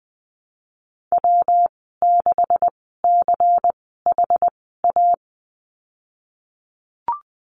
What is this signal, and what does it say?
Morse code, heard as a keyed tone, about 20 wpm, spaced as for 15 wpm, 700 Hz. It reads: W6CHA
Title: 186 Lesson 30g - Letter b - Call Signs 20wpm